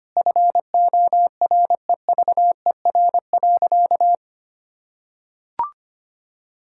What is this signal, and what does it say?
Morse code, heard as a keyed tone, about 25 wpm, 700 Hz. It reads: FOREVER.
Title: Sentences from Top 1600 Words 25wpm